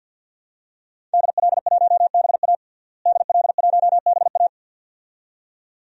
Morse code, heard as a keyed tone, about 52 wpm, 700 Hz. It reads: DL1BA DL1BA
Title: DXCC-100 Call Signs - 5 Characters - Repeated 2x - 50wpm